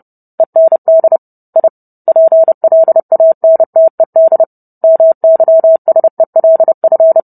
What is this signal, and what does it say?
Morse code, heard as a keyed tone, about 30 wpm, 650 Hz. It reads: END I PLANTED MYSELF